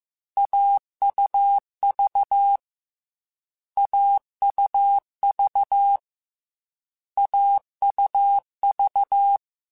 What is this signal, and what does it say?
Morse code, heard as a keyed tone, about 15 wpm, 800 Hz. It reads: AUV AUV AUV
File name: Short Runs / Short Runs-mp3s / AUV AUV4 @13